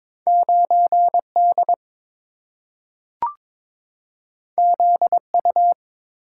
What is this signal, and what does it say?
Morse code, heard as a keyed tone, about 22 wpm, 690 Hz. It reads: 9D ZU
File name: practice / 22wpm / Two Letter-Number Combinations - 22wpm